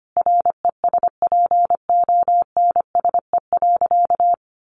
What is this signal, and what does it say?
Morse code, heard as a keyed tone, about 25 wpm, 700 Hz. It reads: RESPONSE.
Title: Sentences from Top 1400 Words 25wpm